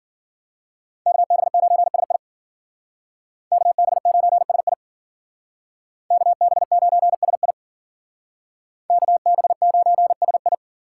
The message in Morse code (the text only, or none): KB9SI KB9SI KB9SI KB9SI